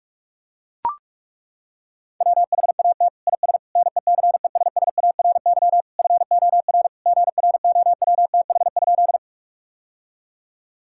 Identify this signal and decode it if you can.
Morse code, heard as a keyed tone, about 42 wpm, 700 Hz. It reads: WHAT IS NECESSARY FOR GROWTH?
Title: Sentences from Top 1000 Words - Review Part 5 - 45wpm